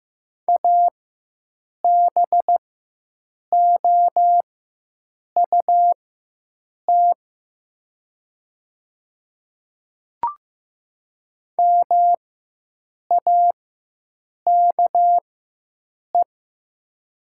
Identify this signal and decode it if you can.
Morse code, heard as a keyed tone, about 15 wpm, spaced as for 7 wpm, 700 Hz. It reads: ABOUT MAKE
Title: 4x Character Spacing - Top 100 Words - 15wpm